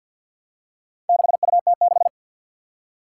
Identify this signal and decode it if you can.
Morse code, heard as a keyed tone, about 45 wpm, 700 Hz. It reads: 6UT6
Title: Four Challenging Letter-Number Combinations 50wpm